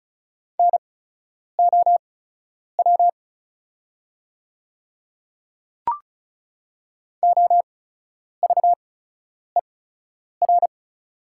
Morse code, heard as a keyed tone, about 36 wpm, spaced as for 10 wpm, 700 Hz. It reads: NOW OVER
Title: 8x Character Spacing - Top 100 Words - 35wpm